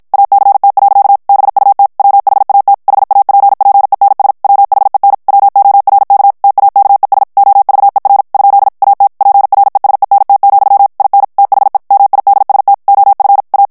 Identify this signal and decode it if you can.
Morse code, heard as a keyed tone, about 55 wpm, 800 Hz. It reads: ACT. BUT WHAT HAPPENS WHEN WORK TAKES OVER ? AT PRESENT, IN THE MIDST OF A
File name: morse mp3 / ANGLAIS texte clair 1 vitesse 50 mots minute